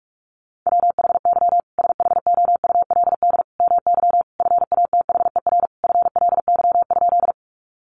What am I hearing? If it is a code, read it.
Morse code, heard as a keyed tone, about 44 wpm, 700 Hz. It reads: WHY SHOULD MY FATHER FLY?
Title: Sentences from Top 500 Words 45wpm